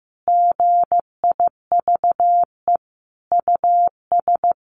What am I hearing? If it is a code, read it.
Morse code, heard as a keyed tone, about 15 wpm, 700 Hz. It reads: GIVE US
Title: Sentences from Top 1100 Words 15wpm